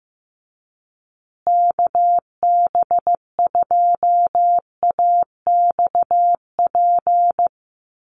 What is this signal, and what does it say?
Morse code, heard as a keyed tone, about 15 wpm, 700 Hz. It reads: KB2AXP
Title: USA Call Signs Repeated 2x 15wpm